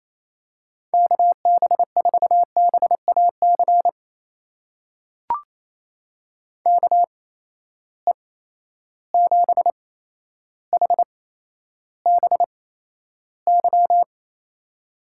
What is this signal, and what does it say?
Morse code, heard as a keyed tone, about 28 wpm, spaced as for 12 wpm, 700 Hz. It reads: KB4BAC KE7HBY
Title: USA Call Signs - 6 Characters - 8x Character Spacing - 28wpm